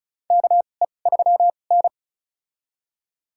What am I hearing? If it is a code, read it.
Morse code, heard as a keyed tone, about 35 wpm, spaced as for 25 wpm, 700 Hz. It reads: KE3N
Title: USA Call Signs - 4 Characters - 2x Character Spacing - 35wpm